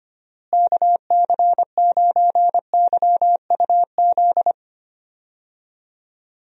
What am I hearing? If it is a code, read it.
Morse code, heard as a keyed tone, about 25 wpm, 700 Hz. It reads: KC9YUZ
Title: USA Call Signs - 6 Characters - Repeated 2x - 25wpm